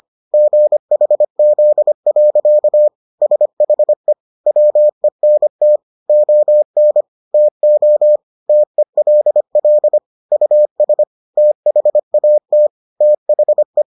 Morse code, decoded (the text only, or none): GHZ. SHE WENT ON TO TELL US THAT THE